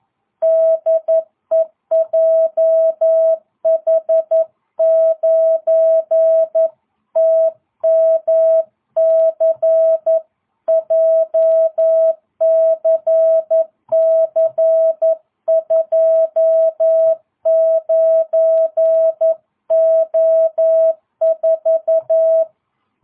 Morse code, heard as a keyed tone, about 11 wpm, 650 Hz. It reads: DE JH9TMC JCC29O4